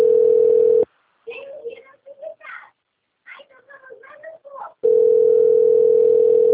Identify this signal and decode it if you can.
Ringback tone (North America), 440 + 480 Hz, with speech.
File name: out- - - - - .34644